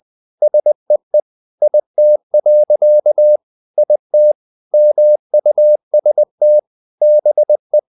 Morse code, heard as a keyed tone, about 20 wpm, 600 Hz. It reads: SEE IT. IT MUST BE
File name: the old man and the sea part 2 20wpm